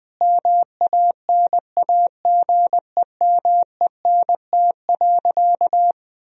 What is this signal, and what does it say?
Morse code, heard as a keyed tone, about 20 wpm, 700 Hz. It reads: MANAGEMENT.
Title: Sentences from Top 1000 Words - Review Part 4 - 20wpm